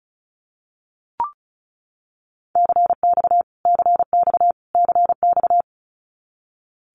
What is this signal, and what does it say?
Morse code, heard as a keyed tone, about 34 wpm, 700 Hz. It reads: CX CX CX
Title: Abbreviations Repeated 3x - 35wpm